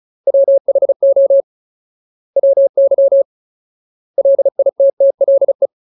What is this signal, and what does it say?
Morse code, heard as a keyed tone, about 36 wpm, 550 Hz. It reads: WHO WY LITTLE
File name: Ultimate Word List / 35WPM Ultimate